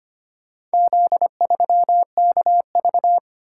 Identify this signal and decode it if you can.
Morse code, heard as a keyed tone, about 25 wpm, 700 Hz. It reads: Z3KV
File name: practice / 25wpm / Four Letter-Number Combinations - 25wpm